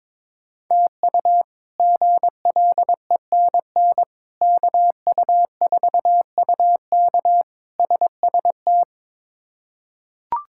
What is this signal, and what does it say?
Morse code, heard as a keyed tone, about 22 wpm, 700 Hz. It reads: TU GLENN KU4UK SST
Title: K1USN SST QSO Practice - 22wpm